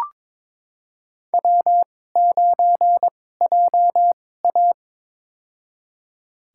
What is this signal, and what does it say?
Morse code, heard as a keyed tone, about 21 wpm, spaced as for 16 wpm, 700 Hz. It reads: W9JA
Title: USA Call Signs - 4 Characters - 2x Character Spacing - 22wpm